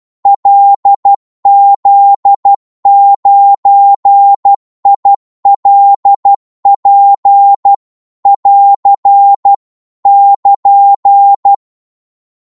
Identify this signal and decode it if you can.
Morse code, heard as a keyed tone, about 12 wpm, 800 Hz. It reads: LZ9ILP <AR> <KN>
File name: Morse / qso-12wpm